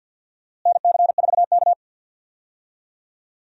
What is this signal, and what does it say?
Morse code, heard as a keyed tone, about 46 wpm, 700 Hz. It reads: NC4K
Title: USA Call Signs - 4 Characters - Repeated 2x - 50wpm